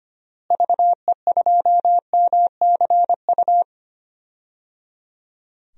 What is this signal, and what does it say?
Morse code, heard as a keyed tone, about 25 wpm, 700 Hz. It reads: VE2MCU
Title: DXCC-100 Call Signs - 6 Characters - Repeated 3x - 25wpm